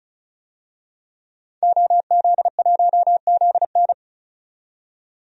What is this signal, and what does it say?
Morse code, heard as a keyed tone, about 34 wpm, 700 Hz. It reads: OZ1ZN
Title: DXCC-100 Call Signs - 5 Characters - 35wpm